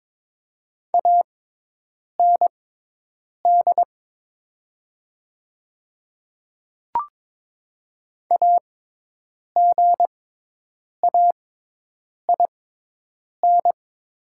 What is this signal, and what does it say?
Morse code, heard as a keyed tone, about 22 wpm, spaced as for 8 wpm, 700 Hz. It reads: AND AGAIN
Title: 6x Character Spacing - Top 500 Words - 22wpm